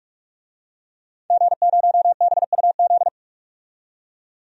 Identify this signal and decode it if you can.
Morse code, heard as a keyed tone, about 47 wpm, 700 Hz. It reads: G0BUZ
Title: DXCC-100 Call Signs - 5 Characters - 45wpm